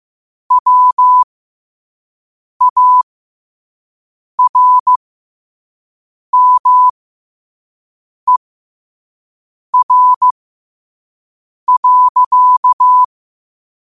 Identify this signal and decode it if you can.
Morse code, heard as a keyed tone, about 15 wpm, spaced as for 5 wpm, 1000 Hz. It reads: WARMER.
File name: qotdfiles / 5wpm